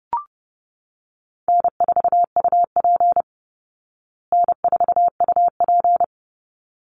Morse code, heard as a keyed tone, about 30 wpm, 700 Hz. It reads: N4UP N4UP
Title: Call Signs Repeated 3x - 30wpm